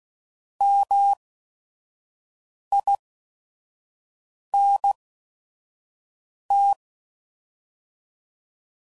Morse code, heard as a keyed tone, about 16 wpm, spaced as for 5 wpm, 800 Hz. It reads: MINT